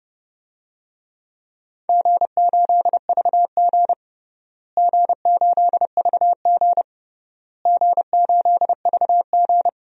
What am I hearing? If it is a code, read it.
Morse code, heard as a keyed tone, about 30 wpm, 700 Hz. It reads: G8VG G8VG G8VG
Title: DXCC-100 Call Signs - 4 Characters - Repeated 3x - 30wpm